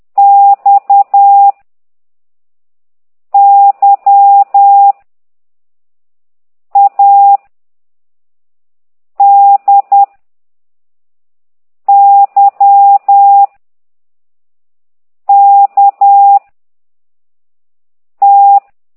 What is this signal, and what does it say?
Morse code, heard as a keyed tone, about 10 wpm, 800 Hz. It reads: X Y A D Y K T
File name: ct2idw / exerc15